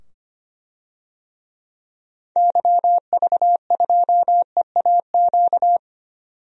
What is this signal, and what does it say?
Morse code, heard as a keyed tone, about 25 wpm, 700 Hz. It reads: YV2EAQ
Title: DXCC-100 Call Signs - 6 Characters - 25wpm